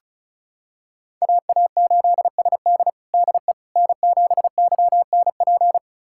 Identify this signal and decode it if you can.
Morse code, heard as a keyed tone, about 36 wpm, 700 Hz. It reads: AA8SD DE N7YNP
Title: USA Callsign DE Callsign - 5 Characters - 35wpm